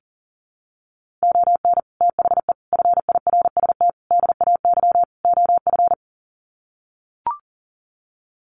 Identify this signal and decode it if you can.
Morse code, heard as a keyed tone, about 40 wpm, 700 Hz. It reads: ON THE FIRST DAY OF
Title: Sets of 5 Words Encore 40wpm